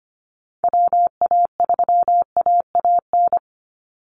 Morse code, heard as a keyed tone, about 25 wpm, 700 Hz. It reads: WA3AAN
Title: Call Signs - 25wpm